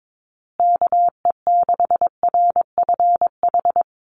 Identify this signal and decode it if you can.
Morse code, heard as a keyed tone, about 22 wpm, 700 Hz. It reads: KE6RFH
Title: USA Call Signs Repeated 3x 22wpm